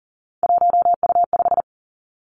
Morse code, heard as a keyed tone, about 38 wpm, 700 Hz. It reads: 1U5
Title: Three Challenging Letter-Number Combinations 40wpm